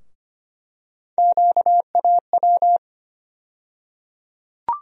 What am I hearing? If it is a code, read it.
Morse code, heard as a keyed tone, about 25 wpm, 700 Hz. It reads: QAW
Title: USA Call Sign Postfixes - 3 Letters 25wpm